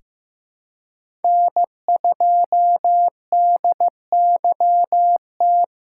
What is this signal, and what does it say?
Morse code, heard as a keyed tone, about 15 wpm, 700 Hz. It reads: N2DYT
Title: K1USN SST QSO Practice - 15wpm